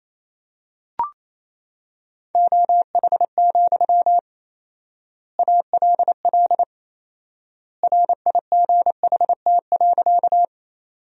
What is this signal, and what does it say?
Morse code, heard as a keyed tone, about 28 wpm, 700 Hz. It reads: OH, ALL RIGHT.
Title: 4x Word-Spacing - Sentences from Top 500 Words - 28wpm